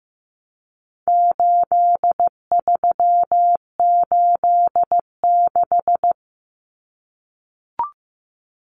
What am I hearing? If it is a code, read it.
Morse code, heard as a keyed tone, about 15 wpm, 700 Hz. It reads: 8386
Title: Numbers - 4-Digit 15wpm